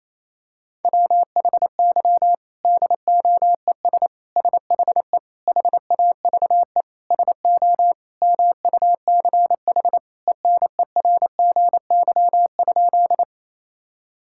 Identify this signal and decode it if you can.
Morse code, heard as a keyed tone, about 28 wpm, 700 Hz. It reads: WHY DOES SHE HAVE SO MUCH ENERGY?
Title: Sentences from Top 1000 Words - Review Part 6 - 28wpm